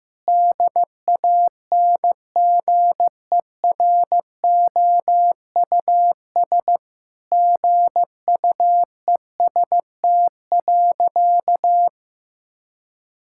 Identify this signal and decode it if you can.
Morse code, heard as a keyed tone, about 16 wpm, 700 Hz. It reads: DANGEROUS GUEST.
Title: Sentences from Top 1600 Words 15wpm